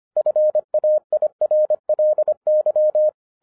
Morse code, heard as a keyed tone, about 25 wpm, 610 Hz. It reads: FAIRLY
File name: 2012-01-mp3 / n06-FIRST FISTS CONTACT-25wpm